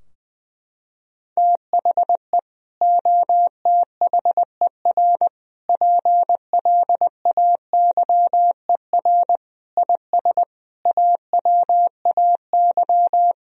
Morse code, heard as a keyed tone, about 20 wpm, 700 Hz. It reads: THE OTHER PLAYER IS AWAY
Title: Sentences from Top 1000 Words - Review Part 4 - 20wpm